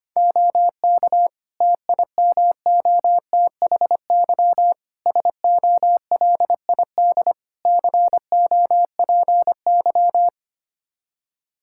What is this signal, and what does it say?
Morse code, heard as a keyed tone, about 25 wpm, 700 Hz. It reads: OK TIMOTHY SOLID COPY
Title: K1USN SST QSO Practice 25wpm